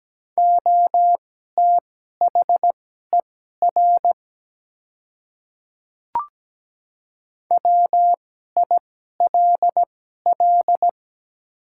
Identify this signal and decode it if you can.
Morse code, heard as a keyed tone, about 17 wpm, spaced as for 12 wpm, 700 Hz. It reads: OTHER WILL
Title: 2x Character Spacing - Top 100 Words - 17wpm